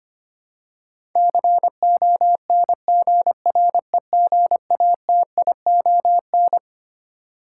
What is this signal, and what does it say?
Morse code, heard as a keyed tone, about 25 wpm, 700 Hz. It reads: CONGREGATION
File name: Twelve Letter Words - 25wpm